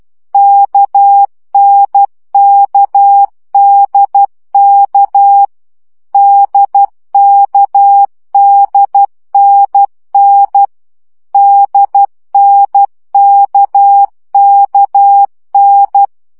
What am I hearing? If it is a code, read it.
Morse code, heard as a keyed tone, about 12 wpm, 800 Hz. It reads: KNKDK DKDNN DNKKN
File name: CW gyakorlas n d k 10 baud